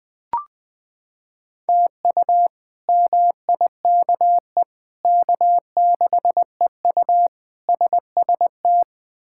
Morse code, heard as a keyed tone, about 20 wpm, 700 Hz. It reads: TU MIKE K6EU SST
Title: K1USN SST QSO Practice - 20wpm